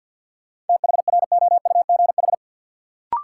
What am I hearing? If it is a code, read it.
Morse code, heard as a keyed tone, about 52 wpm, 700 Hz. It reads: THROUGH
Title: Sets of 2 Words - New Word Drill - 50wpm